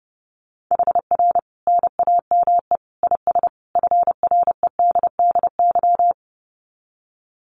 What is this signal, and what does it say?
Morse code, heard as a keyed tone, about 30 wpm, 700 Hz. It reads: HR NAME IS FREDDY
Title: Common QSO Phrases 30wpm